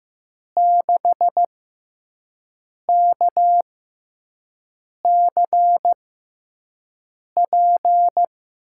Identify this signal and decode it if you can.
Morse code, heard as a keyed tone, about 15 wpm, spaced as for 5 wpm, 700 Hz. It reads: 6KCP
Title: USA Call Signs - 6x Character Spacing - 15wpm